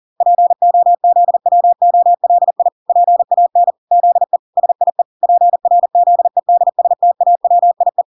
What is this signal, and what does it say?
Morse code, heard as a keyed tone, about 40 wpm, 700 Hz. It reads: POZWOLI PAN ZE SIE PRZEDSTAWIE